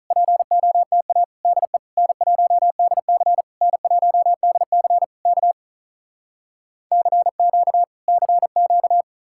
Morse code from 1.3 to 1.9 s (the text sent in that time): DE